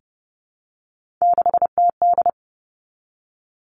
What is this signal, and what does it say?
Morse code, heard as a keyed tone, about 30 wpm, 700 Hz. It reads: 6TD
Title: Three Challenging Letter-Number Combinations 30wpm